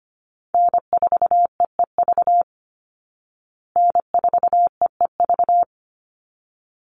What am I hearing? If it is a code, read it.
Morse code, heard as a keyed tone, about 25 wpm, 700 Hz. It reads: N4EEV N4EEV
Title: Call Signs Repeated 3x - 25wpm